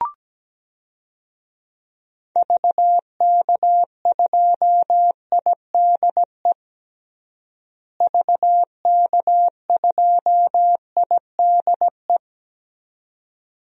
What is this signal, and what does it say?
Morse code, heard as a keyed tone, about 17 wpm, 700 Hz. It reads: VK2IDE VK2IDE
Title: DXCC-100 Call Signs - 6 Characters - Repeated 2x - 17wpm